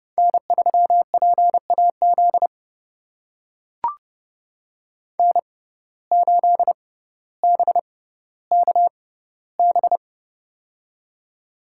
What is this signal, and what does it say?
Morse code, heard as a keyed tone, about 30 wpm, spaced as for 15 wpm, 700 Hz. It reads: N3PAZ N8BKB
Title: USA Call Signs - 5 Characters - 6x Character Spacing - 30wpm